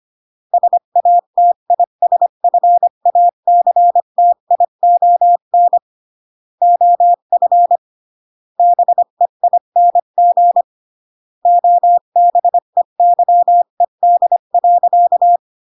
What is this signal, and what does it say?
Morse code, heard as a keyed tone, about 25 wpm, 700 Hz. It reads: SATISFACTION OF BEING OBEYED.
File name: CW / jodorowsky / 82 commandments 25-20